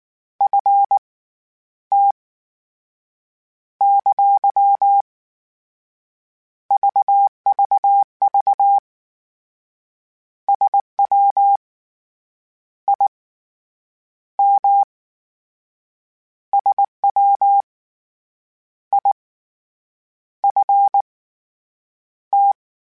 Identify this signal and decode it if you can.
Morse code, heard as a keyed tone, about 19 wpm, spaced as for 8 wpm, 800 Hz. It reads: FT ! VVV SWIM SWIFT